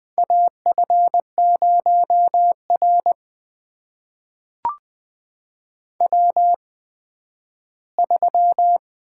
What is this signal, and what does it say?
Morse code, heard as a keyed tone, about 20 wpm, spaced as for 12 wpm, 700 Hz. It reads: AF0R W3
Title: USA Call Signs - 8x Character Spacing - 20wpm